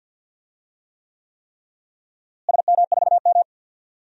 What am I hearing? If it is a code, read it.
Morse code, heard as a keyed tone, about 47 wpm, 700 Hz. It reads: SM4M
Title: DXCC-100 Call Signs - 4 Characters - Repeated 3x - 50wpm